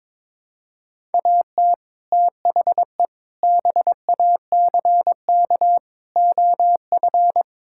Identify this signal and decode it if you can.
Morse code, heard as a keyed tone, about 21 wpm, 700 Hz. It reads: AT THE BACK OF
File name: practice / 22wpm / Build A Phrase - 5-Word Phrases - 22wpm